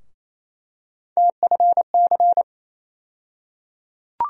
Morse code, heard as a keyed tone, about 28 wpm, 700 Hz. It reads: TFC
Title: USA Call Sign Postfixes - 3 Letters 28wpm